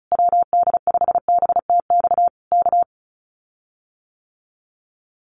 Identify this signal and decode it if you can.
Morse code, heard as a keyed tone, about 36 wpm, 700 Hz. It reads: WD5BTX K
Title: Common QSO Phrases Encore 35wpm